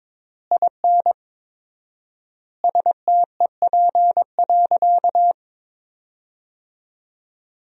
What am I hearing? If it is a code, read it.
Morse code, heard as a keyed tone, about 22 wpm, 690 Hz. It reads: IN STEP.
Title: 4x Word-Spacing - Sentences from Top 400 Words - 22wpm